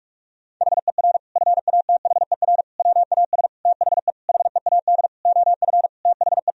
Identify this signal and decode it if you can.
Morse code, heard as a keyed tone, about 46 wpm, 700 Hz. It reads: HER FATHER WAS THE HEAD OF THE
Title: Sentences from Top 1000 Words - Review Part 4 - 45wpm